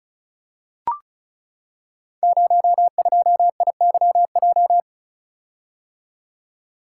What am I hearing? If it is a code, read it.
Morse code, heard as a keyed tone, about 36 wpm, 700 Hz. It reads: 02IYJ
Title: Code Groups - 5 Letter:Numbers - 35wpm